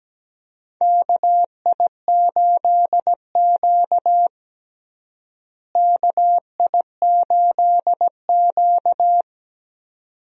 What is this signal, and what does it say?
Morse code, heard as a keyed tone, about 17 wpm, 700 Hz. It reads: KI8Q KI8Q
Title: USA Call Signs - 4 Characters - Repeated 2x - 17wpm